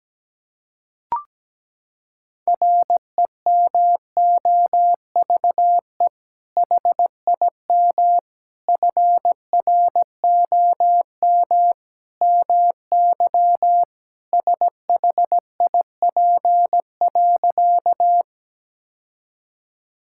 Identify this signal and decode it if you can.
Morse code, heard as a keyed tone, about 17 wpm, 700 Hz. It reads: REMOVE HIM FROM MY SHIP.
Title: Sentences from Top 1000 Words - Review Part 5 - 17wpm